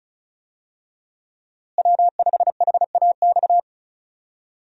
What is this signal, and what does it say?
Morse code, heard as a keyed tone, about 36 wpm, 700 Hz. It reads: W5HAX
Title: USA Call Signs - 5 Characters - Repeated 3x - 35wpm